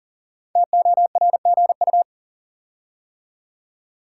Morse code, heard as a keyed tone, about 38 wpm, 690 Hz. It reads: TORGU
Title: Code Groups - 5 Letters - 40wpm